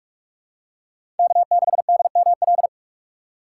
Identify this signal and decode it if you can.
Morse code, heard as a keyed tone, about 46 wpm, 700 Hz. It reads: K6DML
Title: USA Call Signs - 5 Characters - 8x Character Spacing - 45wpm